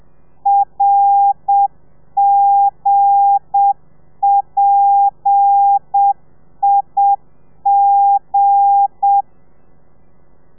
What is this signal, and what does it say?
Morse code, heard as a keyed tone, about 7 wpm, 800 Hz. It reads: RGPIG
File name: telegrafia / i2tfj / Lez8 30